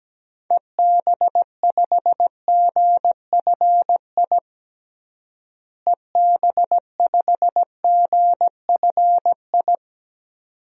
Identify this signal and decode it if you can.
Morse code, heard as a keyed tone, about 17 wpm, 700 Hz. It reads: EB5GFI EB5GFI
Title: DXCC-100 Call Signs - 6 Characters - Repeated 2x - 17wpm